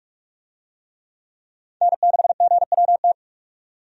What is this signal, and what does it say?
Morse code, heard as a keyed tone, about 46 wpm, 700 Hz. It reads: N6GWT
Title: USA Call Signs - 5 Characters - 45wpm